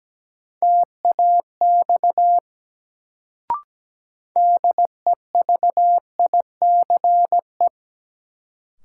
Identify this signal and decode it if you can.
Morse code, heard as a keyed tone, about 17 wpm, 700 Hz. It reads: TAX DEVICE
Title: Sentences from Top 1500 Words - New Word Drill Alt - 17wpm